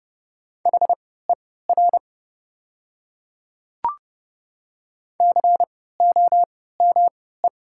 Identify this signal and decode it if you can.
Morse code, heard as a keyed tone, about 30 wpm, spaced as for 17 wpm, 700 Hz. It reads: HER COME
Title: 3x Character Spacing - Top 100 Words - 30wpm